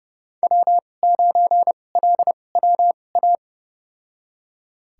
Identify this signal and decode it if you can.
Morse code, heard as a keyed tone, about 32 wpm, spaced as for 22 wpm, 700 Hz. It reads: W9LWA
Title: USA Call Signs - 2x Character Spacing - 30wpm